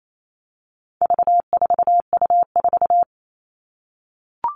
Morse code, heard as a keyed tone, about 28 wpm, 700 Hz. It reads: V4U4
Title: All Cut Numbers - 4-Digit 28wpm